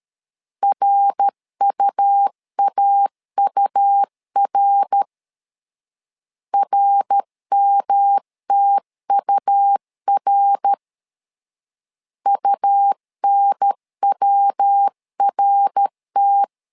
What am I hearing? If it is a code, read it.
Morse code, heard as a keyed tone, about 13 wpm, 800 Hz. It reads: RUAUR RMTUR UNWRT
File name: LEZIONI / 3 MWRO / 3 MWR 12wpm 1711